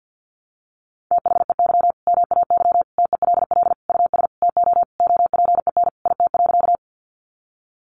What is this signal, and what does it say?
Morse code, heard as a keyed tone, about 50 wpm, 700 Hz. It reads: THEY MAY TELL US TO OPEN IT.
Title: Sentences from Top 300 Words 50wpm